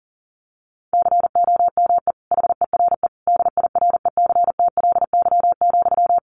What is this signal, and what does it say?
Morse code, heard as a keyed tone, about 40 wpm, 700 Hz. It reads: COME HERE DIRECTLY,
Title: Sentences from Top 800 Words 40wpm